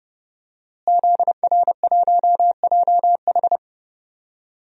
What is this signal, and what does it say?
Morse code, heard as a keyed tone, about 31 wpm, 700 Hz. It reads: ZR1JH
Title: DXCC-100 Call Signs - 5 Characters - Repeated 2x - 30wpm